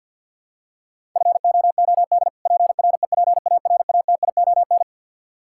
Morse code, heard as a keyed tone, about 52 wpm, 700 Hz. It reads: FOOD PREPARATION
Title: Practice QSO - Exchange 3 - More Info - 50wpm